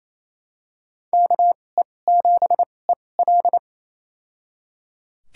Morse code, heard as a keyed tone, about 28 wpm, spaced as for 20 wpm, 700 Hz. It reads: KE7EL
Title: USA Call Signs - 5 Characters - 2x Character Spacing - 28wpm